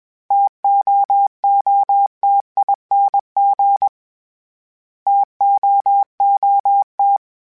Morse code, heard as a keyed tone, about 21 wpm, 800 Hz. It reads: TOOTING TOOT
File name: Variable Spacing / Variable Spacing-mp3s / TOs @20